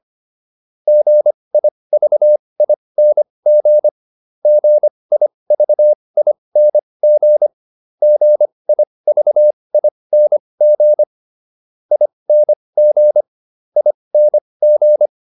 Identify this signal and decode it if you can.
Morse code, heard as a keyed tone, about 25 wpm, spaced as for 20 wpm, 600 Hz. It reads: GIVING GIVING GIVING ING ING